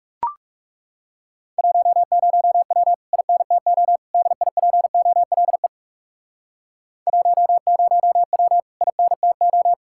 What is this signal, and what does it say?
Morse code, heard as a keyed tone, about 40 wpm, 700 Hz. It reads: 10W INTO DIPOLE 10W INTO